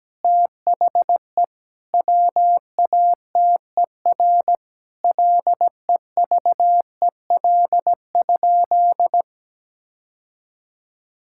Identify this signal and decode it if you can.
Morse code, heard as a keyed tone, about 17 wpm, 700 Hz. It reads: THE WATER LEVEL?
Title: Sentences from Top 1000 Words - Review Part 5 - 17wpm